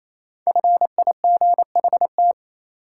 Morse code, heard as a keyed tone, about 28 wpm, 700 Hz. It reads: FIGHT
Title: Five Letter Words - 28wpm